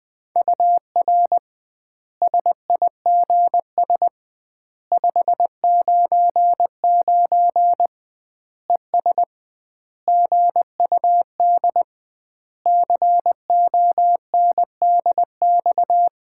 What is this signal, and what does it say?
Morse code, heard as a keyed tone, about 20 wpm, 700 Hz. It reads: UR SIGS 599 ES GUD CONDX
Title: Common QSO Phrases - 2x Word-Spacing - 20wpm